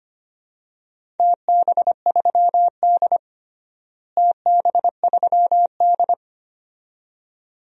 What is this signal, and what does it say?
Morse code, heard as a keyed tone, about 25 wpm, 700 Hz. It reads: TB3D TB3D
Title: DXCC-100 Call Signs - 4 Characters - Repeated 2x - 25wpm